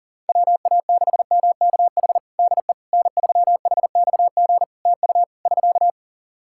Morse code, heard as a keyed tone, about 37 wpm, 700 Hz. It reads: WA6MKH DE N3HXG TU <SK>
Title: Practice QSO - Exchange 4 - Ending - 40wpm